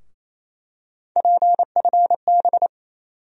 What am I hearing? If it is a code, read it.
Morse code, heard as a keyed tone, about 28 wpm, 700 Hz. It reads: PFB
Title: USA Call Sign Postfixes - 3 Letters 28wpm